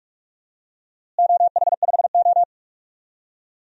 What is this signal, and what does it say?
Morse code, heard as a keyed tone, about 46 wpm, 700 Hz. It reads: OH5O
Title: DXCC-100 Call Signs - 4 Characters - Repeated 2x - 45wpm